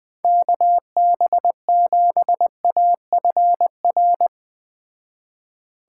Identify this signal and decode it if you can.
Morse code, heard as a keyed tone, about 19 wpm, 700 Hz. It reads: KB7AFR
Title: Speed Racing - USA 2x3 Call Signs 17wpm